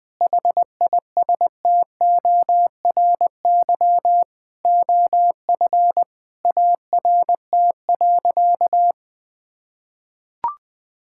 Morse code, heard as a keyed tone, about 21 wpm, 700 Hz. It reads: HISTORY OF ART.
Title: Sentences with 4-Word Phrases - 20wpm